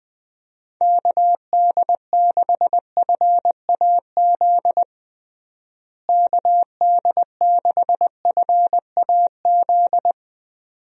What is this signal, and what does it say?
Morse code, heard as a keyed tone, about 20 wpm, 700 Hz. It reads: KD6FAZ KD6FAZ
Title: USA Call Signs - 6 Characters - Repeated 2x - 20wpm